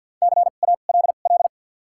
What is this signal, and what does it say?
Morse code, heard as a keyed tone, about 53 wpm, spaced as for 37 wpm, 700 Hz. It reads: CALL
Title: 2x Character Spacing - Top 100 Words - 50wpm